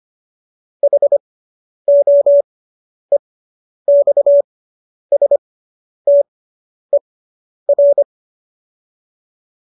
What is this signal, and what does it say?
Morse code, heard as a keyed tone, about 25 wpm, spaced as for 10 wpm, 570 Hz. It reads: HOEXSTER